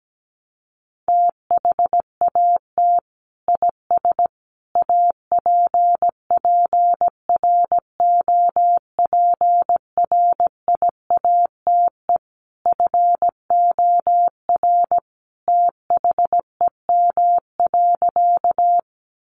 Morse code, heard as a keyed tone, about 17 wpm, 700 Hz. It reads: THAT IS APPROPRIATE FOR THEM.
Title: Sentences from Top 900 Words 17wpm